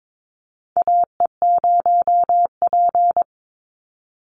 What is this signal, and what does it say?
Morse code, heard as a keyed tone, about 22 wpm, 700 Hz. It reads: AE0P
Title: Call Signs - 22wpm